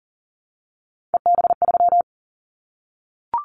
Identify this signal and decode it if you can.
Morse code, heard as a keyed tone, about 40 wpm, 700 Hz. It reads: EB3
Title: Callsign Prefixes for Top 25 DXCC - 40wpm